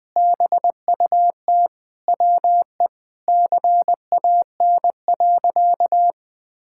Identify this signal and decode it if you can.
Morse code, heard as a keyed tone, about 21 wpm, 700 Hz. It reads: BUT WE CAN.
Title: Sentences with 2-Word Phrases - 20wpm